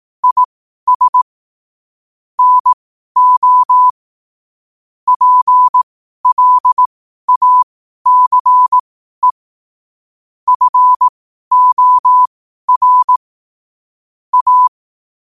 Morse code, heard as a keyed tone, about 18 wpm, spaced as for 12 wpm, 1000 Hz. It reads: IS NO PLACE FOR A
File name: qotdfiles / 12wpm